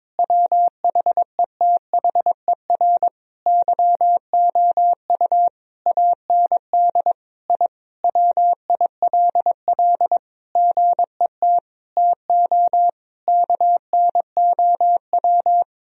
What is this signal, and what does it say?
Morse code, heard as a keyed tone, about 22 wpm, 700 Hz. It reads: WHETHER YOU AND I WILL GET TO KNOW
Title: Sentences from Top 1000 Words - Review Part 3 - 22wpm